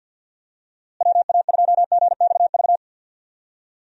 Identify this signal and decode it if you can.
Morse code, heard as a keyed tone, about 48 wpm, 700 Hz. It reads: WA2GXV
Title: USA Call Signs - 6 Characters - Repeated 2x - 50wpm